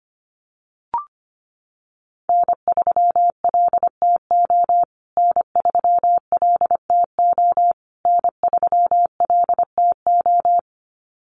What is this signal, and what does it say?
Morse code, heard as a keyed tone, about 25 wpm, 700 Hz. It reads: N3LTO N3LTO N3LTO
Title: USA Call Signs Repeated 3x 25wpm